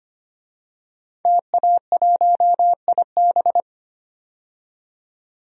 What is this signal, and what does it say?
Morse code, heard as a keyed tone, about 25 wpm, 700 Hz. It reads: TA1IB
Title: DXCC-100 Call Signs - 5 Characters - 25wpm